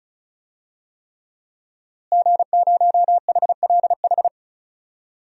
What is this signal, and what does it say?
Morse code, heard as a keyed tone, about 36 wpm, 700 Hz. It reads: G0HLH
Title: DXCC-100 Call Signs - 5 Characters - Repeated 3x - 35wpm